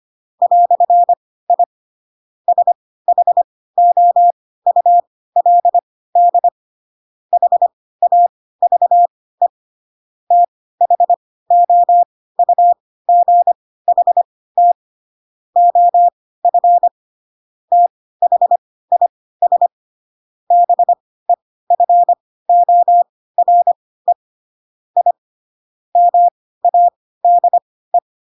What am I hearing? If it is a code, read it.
Morse code, heard as a keyed tone, about 25 wpm, spaced as for 16 wpm, 700 Hz. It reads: "I SHOULD HAVE THOUGHT OF THIS BEFORE I MADE
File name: Pinocchio / Pinocchio ch3 10 25 ik1hge